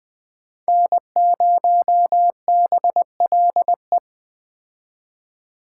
Morse code, heard as a keyed tone, about 21 wpm, 700 Hz. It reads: N0BLE
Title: Speed Racing - USA Call Signs 17wpm